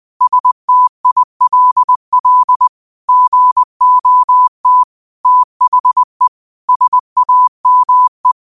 Morse code, heard as a keyed tone, about 20 wpm, 1000 Hz. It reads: STILL GOT THE SAME